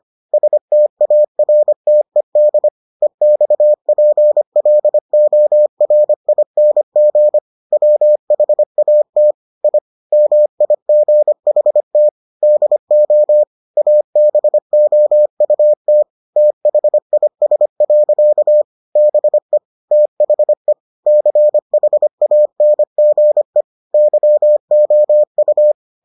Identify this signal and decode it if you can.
Morse code, heard as a keyed tone, about 25 wpm, 600 Hz. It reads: STARTED EXPLORING WHAT I MIGHT DO ABOUT THIS. BE THE CHANGE YOU